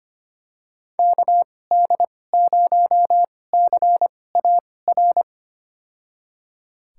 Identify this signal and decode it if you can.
Morse code, heard as a keyed tone, about 25 wpm, spaced as for 18 wpm, 700 Hz. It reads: KD0CAR
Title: USA Call Signs - 6 Characters - 2x Character Spacing - 25wpm